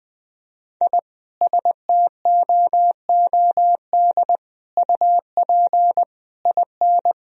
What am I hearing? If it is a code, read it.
Morse code, heard as a keyed tone, about 20 wpm, 700 Hz. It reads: I STOOD UP IN
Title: Sentences from Top 1000 Words - Review Part 2 - 20wpm